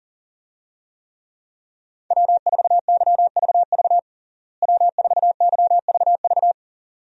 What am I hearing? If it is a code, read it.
Morse code, heard as a keyed tone, about 40 wpm, 700 Hz. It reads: W4YVV W4YVV
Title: USA Call Signs - 5 Characters - Repeated 2x - 40wpm